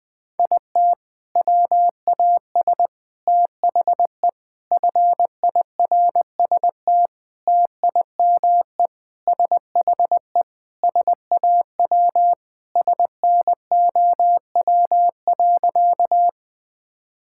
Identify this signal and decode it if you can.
Morse code, heard as a keyed tone, about 20 wpm, 700 Hz. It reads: IT WAS THE FIRST TIME SHE SAW SNOW.
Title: Sentences with 5-Word Phrases - Spoken Beginning - 20wpm